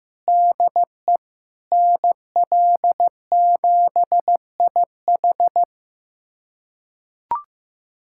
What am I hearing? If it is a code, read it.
Morse code, heard as a keyed tone, about 15 wpm, 700 Hz. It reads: DE NL7IH
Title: USA Callsign DE Callsign 15wpm